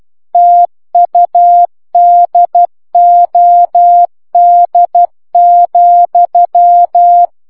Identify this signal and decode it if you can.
Morse code, heard as a keyed tone, about 12 wpm, 700 Hz. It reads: TUDOD,